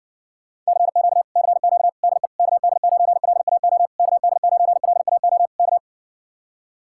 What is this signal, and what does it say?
Morse code, heard as a keyed tone, about 70 wpm, 700 Hz. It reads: CQ CQ DE KD9LAO KD9LAO K